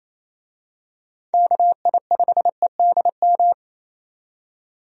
Morse code, heard as a keyed tone, about 28 wpm, 700 Hz. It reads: KI5EDM
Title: USA Call Signs - 6 Characters - 28wpm